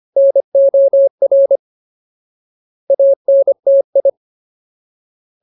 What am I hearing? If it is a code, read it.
Morse code, heard as a keyed tone, about 25 wpm, 550 Hz. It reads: NOR ANTI
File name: Ultimate Word List / 25WPM Ultimate